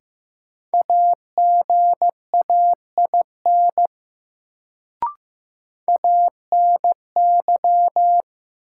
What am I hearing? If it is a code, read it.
Morse code, heard as a keyed tone, about 15 wpm, 700 Hz. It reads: AGAIN ANY
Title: Sentences from Top 200 Words - New Word Drill - 15wpm